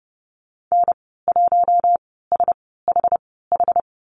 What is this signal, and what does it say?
Morse code, heard as a keyed tone, about 30 wpm, spaced as for 17 wpm, 700 Hz. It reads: N1SHH
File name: Lesson 14g - Number 2 - Call Signs 30wpm